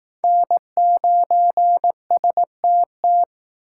Morse code, heard as a keyed tone, about 19 wpm, 700 Hz. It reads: N9STT
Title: Speed Racing - USA 1x3 Call Signs 15wpm